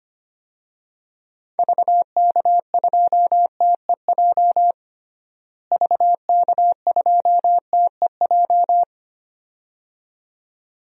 Morse code, heard as a keyed tone, about 25 wpm, 700 Hz. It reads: VK2TEJ VK2TEJ
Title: DXCC-100 Call Signs - 6 Characters - Repeated 2x - 25wpm